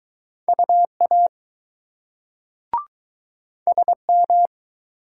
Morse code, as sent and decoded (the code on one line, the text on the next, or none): ..- .- / ... --
UA SM